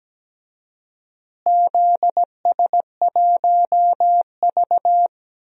Morse code, heard as a keyed tone, about 17 wpm, 700 Hz. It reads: ZS1V
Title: DXCC-100 Call Signs - 4 Characters - Repeated 2x - 17wpm